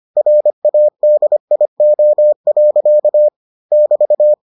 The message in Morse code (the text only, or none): RADIO. <BT>